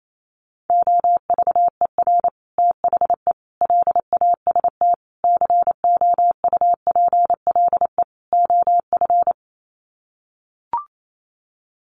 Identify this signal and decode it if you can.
Morse code, heard as a keyed tone, about 28 wpm, 700 Hz. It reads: OVER THE LAST COUPLE OF
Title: Sets of 5 Words Encore 28wpm